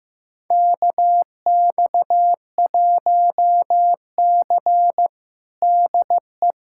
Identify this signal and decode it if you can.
Morse code, heard as a keyed tone, about 15 wpm, 700 Hz. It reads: KX1C DE
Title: USA Callsign DE Callsign - 4 Characters - 15wpm